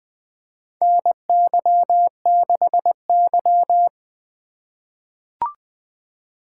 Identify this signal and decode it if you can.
Morse code, heard as a keyed tone, about 20 wpm, 700 Hz. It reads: NY6Y
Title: K1USN SST QSO Practice - 20wpm